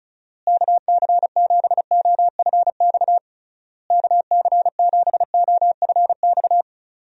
Morse code, heard as a keyed tone, about 34 wpm, 700 Hz. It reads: KC7OFX KC7OFX
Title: USA Call Signs - 6 Characters - Repeated 3x - 35wpm